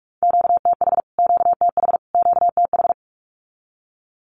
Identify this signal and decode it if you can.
Morse code, heard as a keyed tone, about 44 wpm, 700 Hz. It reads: QTH QTH QTH
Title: Q-Codes Repeated 3x 45wpm